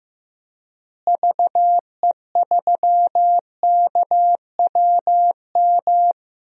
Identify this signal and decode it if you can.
Morse code, heard as a keyed tone, about 15 wpm, 700 Hz. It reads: VE3KWM
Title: DXCC-100 Call Signs - 6 Characters - Repeated 3x - 15wpm